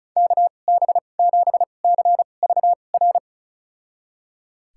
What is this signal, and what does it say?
Morse code, heard as a keyed tone, about 36 wpm, spaced as for 25 wpm, 700 Hz. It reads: KB7CVR
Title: USA Call Signs - 6 Characters - 2x Character Spacing - 35wpm